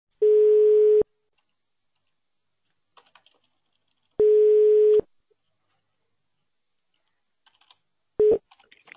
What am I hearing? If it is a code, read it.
Ringback tone (425 Hz, 1 s on, 3 s off).